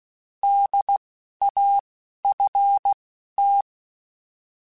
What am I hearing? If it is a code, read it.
Morse code, heard as a keyed tone, about 16 wpm, spaced as for 12 wpm, 800 Hz. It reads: DAFT
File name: Alphabetical / Alphabetical mp3s / Ds @10